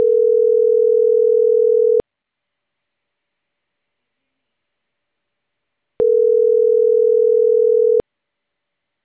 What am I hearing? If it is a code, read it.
Ringback tone (North America), 440 + 480 Hz.